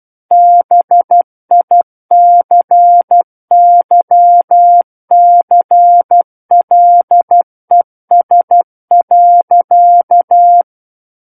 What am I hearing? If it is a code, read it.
Morse code, heard as a keyed tone, about 12 wpm, 700 Hz. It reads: BICYCLES.
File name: Book One / 03-Chapter 3 - 12 WPM